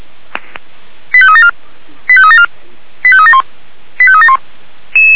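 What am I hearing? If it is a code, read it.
Five-tone selective call (ZVEI), heard as quick runs of five tones, tones of about 70 ms. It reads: 85374 85374 85371 85371
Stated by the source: ZVEI five-tone selective call received with an SDR, Feuerwehr Orlamünde fire brigade.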